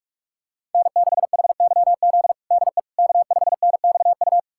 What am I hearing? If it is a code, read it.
Morse code, heard as a keyed tone, about 44 wpm, 700 Hz. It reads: N6HYZ DE K5NXU
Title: USA Callsign DE Callsign - 5 Characters - 45wpm